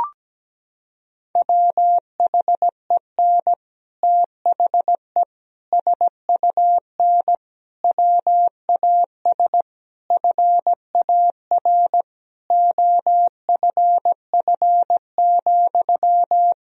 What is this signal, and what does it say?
Morse code, heard as a keyed tone, about 17 wpm, 700 Hz. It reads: WHEN THE SUN WAS FAR OFF,
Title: Sentences from Top 1000 Words - Review Part 3 - 17wpm